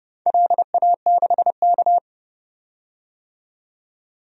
Morse code, heard as a keyed tone, about 30 wpm, 700 Hz. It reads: LA6K
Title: DXCC-100 Call Signs - 4 Characters - 30wpm